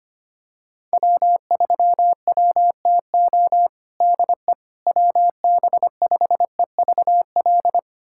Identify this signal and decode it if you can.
Morse code, heard as a keyed tone, about 25 wpm, 700 Hz. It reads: W3WTO DE WB5EVL
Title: USA Callsign DE Callsign 25wpm